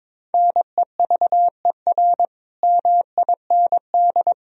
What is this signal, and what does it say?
Morse code, heard as a keyed tone, about 22 wpm, 700 Hz. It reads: NEVER MIND
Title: Sentences from Top 1000 Words - Review Part 4 - 22wpm